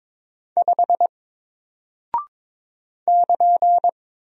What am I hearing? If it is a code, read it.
Morse code, heard as a keyed tone, about 22 wpm, 700 Hz. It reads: 5 <KN>
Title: Single Character (Letter:Number:Punctuation:Prosign) Alt 22wpm